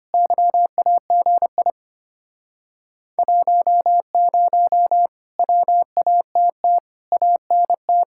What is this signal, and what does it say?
Morse code, heard as a keyed tone, about 32 wpm, 700 Hz. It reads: YAGI 10 WATT ANT